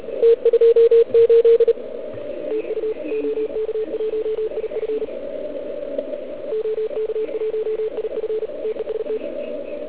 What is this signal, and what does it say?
Morse code, heard as a keyed tone, about 36 wpm, 440 Hz, with radio noise. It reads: T28 TU OK1IF OK1IF TEST